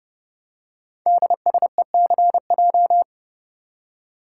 Morse code, heard as a keyed tone, about 30 wpm, 700 Hz. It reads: DSECJ
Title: Code Groups - 5 Letters - 30wpm